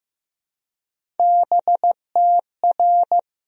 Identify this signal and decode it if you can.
Morse code, heard as a keyed tone, about 14 wpm, 700 Hz. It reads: BTR
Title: Sentences from Top 400 Words and Abbreviations 15wpm